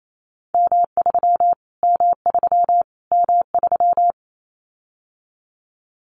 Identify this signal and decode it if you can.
Morse code, heard as a keyed tone, about 28 wpm, 700 Hz. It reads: M3 M3 M3
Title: Callsign Prefixes for Top 25 DXCC - Repeated 3x - 28wpm